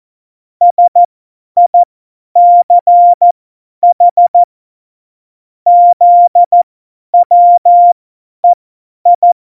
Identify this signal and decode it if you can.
Morse code, heard as a keyed tone, about 14 wpm, spaced as for 10 wpm, 700 Hz. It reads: SICH ZWEI
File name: Morsekurs / Witze / Witz 07 Juli 2020, 50 F70